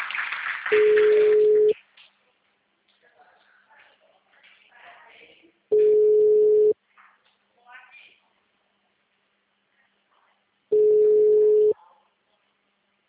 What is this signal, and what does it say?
Ringback tone (Europe, 425 Hz), with speech.
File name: out- -2012- - - .37